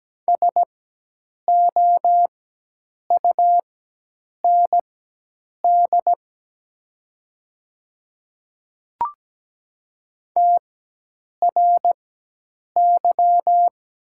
Morse code, heard as a keyed tone, about 17 wpm, spaced as for 8 wpm, 700 Hz. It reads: SOUND TRY
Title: 4x Character Spacing - Top 200 Words - 17wpm